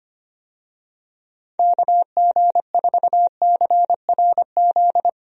Morse code, heard as a keyed tone, about 25 wpm, 700 Hz. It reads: KG4CRZ
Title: USA Call Signs - 6 Characters - Repeated 2x - 25wpm